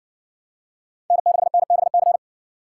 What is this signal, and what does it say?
Morse code, heard as a keyed tone, about 62 wpm, 700 Hz. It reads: N6NBC